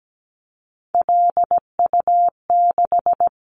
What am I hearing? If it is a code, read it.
Morse code, heard as a keyed tone, about 17 wpm, 700 Hz. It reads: LU6
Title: Callsign Prefixes for Top 100 DXCC - 17wpm